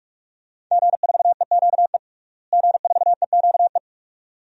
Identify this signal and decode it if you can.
Morse code, heard as a keyed tone, about 46 wpm, 700 Hz. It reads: G4EQE G4EQE
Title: DXCC-100 Call Signs - 5 Characters - Repeated 2x - 45wpm